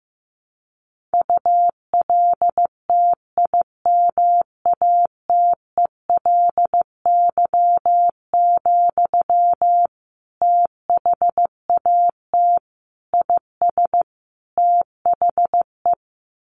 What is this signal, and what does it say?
Morse code, heard as a keyed tone, about 15 wpm, 700 Hz. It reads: ULTIMATELY, THAT IS THE
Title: Sentences from Top 1200 Words 15wpm